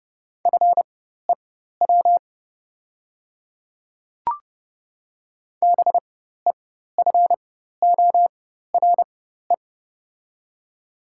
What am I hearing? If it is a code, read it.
Morse code, heard as a keyed tone, about 30 wpm, spaced as for 14 wpm, 700 Hz. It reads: FEW BEFORE
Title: 4x Character Spacing - Top 300 Words - 30wpm